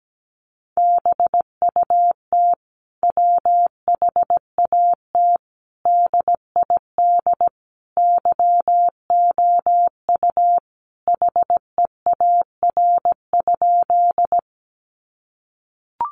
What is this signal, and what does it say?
Morse code, heard as a keyed tone, about 17 wpm, 700 Hz. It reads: BUT WHAT DID YOU HEAR?
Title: Sentences from Top 300 Words 17wpm